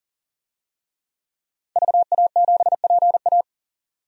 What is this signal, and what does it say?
Morse code, heard as a keyed tone, about 40 wpm, 700 Hz. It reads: VA7PA
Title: DXCC-100 Call Signs - 5 Characters - 40wpm